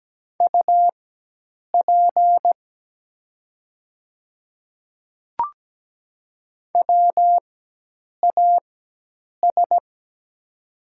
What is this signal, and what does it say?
Morse code, heard as a keyed tone, about 17 wpm, spaced as for 8 wpm, 700 Hz. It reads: UP WAS
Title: 4x Character Spacing - Top 100 Words - 17wpm